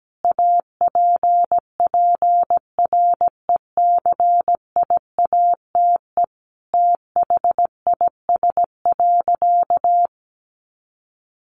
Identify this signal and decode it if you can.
Morse code, heard as a keyed tone, about 17 wpm, 700 Hz. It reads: APPRECIATE THIS.
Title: Sentences from Top 1300 Words 17wpm